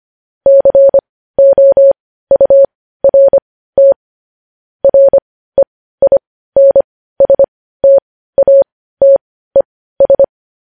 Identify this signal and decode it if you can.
Morse code, heard as a keyed tone, about 25 wpm, spaced as for 15 wpm, 550 Hz. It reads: COURT REINSTATES
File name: podcast / News headlines-1x-25x15